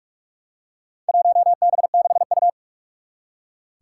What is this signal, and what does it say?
Morse code, heard as a keyed tone, about 44 wpm, 700 Hz. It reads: 1B6U
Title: Four Challenging Letter-Number Combinations 45wpm